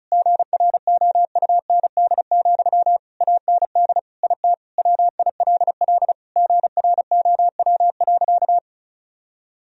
Morse code, heard as a keyed tone, about 36 wpm, 700 Hz. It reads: GROUND, AND IT WILL GROW.
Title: Sentences from Top 1000 Words - Review Part 2 - 35wpm